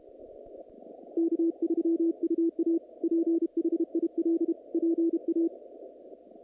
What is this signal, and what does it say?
Morse code, heard as a keyed tone, about 31 wpm, 330 Hz. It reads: K3UA PHIL PA
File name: contests / naqp-cw-k5tr-op-k5pi- -2300